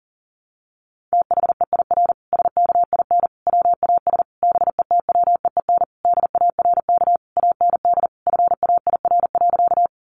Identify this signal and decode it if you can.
Morse code, heard as a keyed tone, about 40 wpm, 700 Hz. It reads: THEIR SKIN WAS BETWEEN DARK AND FAIR.
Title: Sentences from Top 1300 Words 40wpm